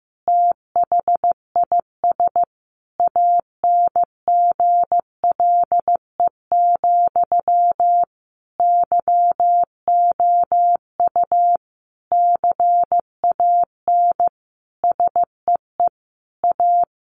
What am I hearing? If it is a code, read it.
Morse code, heard as a keyed tone, about 15 wpm, 700 Hz. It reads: THIS ANGLE, YOU CAN SEE A
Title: Sentences from Top 1500 Words 15wpm